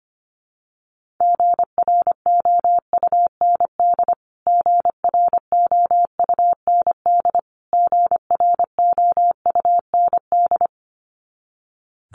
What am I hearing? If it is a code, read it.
Morse code, heard as a keyed tone, about 25 wpm, 700 Hz. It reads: GROUND GROUND GROUND
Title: Top 500 Words Repeated 25wpm